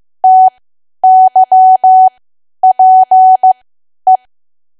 Morse code, heard as a keyed tone, about 15 wpm, spaced as for 10 wpm, 750 Hz. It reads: TYPE
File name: cwa / W1AW-Level2 / 10WPM